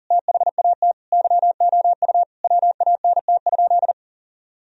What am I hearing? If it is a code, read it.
Morse code, heard as a keyed tone, about 37 wpm, 700 Hz. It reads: THAT YOU WANT?
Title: Sentences from Top 1000 Words - Review Part 4 - 40wpm